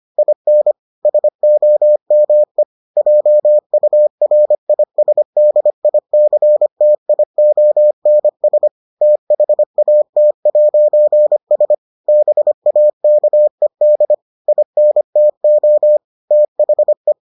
Morse code, heard as a keyed tone, about 25 wpm, 600 Hz. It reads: IN SOME JURISDICTIONS THAT'S BAKED INTO THE